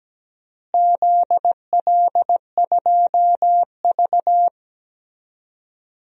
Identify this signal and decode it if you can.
Morse code, heard as a keyed tone, about 17 wpm, 700 Hz. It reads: ZL2V
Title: DXCC-100 Call Signs - 4 Characters - Repeated 2x - 17wpm